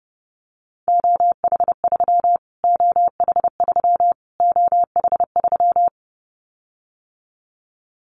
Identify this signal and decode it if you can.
Morse code, heard as a keyed tone, about 30 wpm, 700 Hz. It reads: OH3 OH3 OH3
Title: Callsign Prefixes for Top 25 DXCC - Repeated 3x - 30wpm